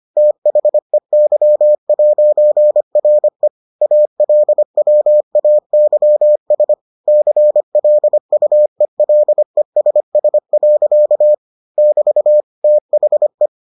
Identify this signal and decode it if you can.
Morse code, heard as a keyed tone, about 25 wpm, 600 Hz. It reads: THEY'RE ALWAYS CLUELESS. <BT> THE